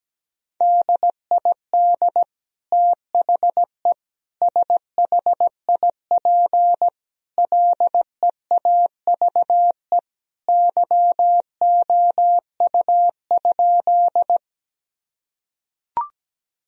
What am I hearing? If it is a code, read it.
Morse code, heard as a keyed tone, about 17 wpm, 700 Hz. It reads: DID THE SHIP LEAVE YOU?
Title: Sentences from Top 1000 Words - Review Part 2 - 17wpm